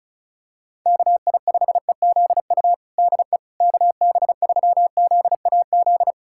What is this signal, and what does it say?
Morse code, heard as a keyed tone, about 34 wpm, 700 Hz. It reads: KI5EZU DE KB3ZAZ
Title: USA Callsign DE Callsign - 6 Characters - 35wpm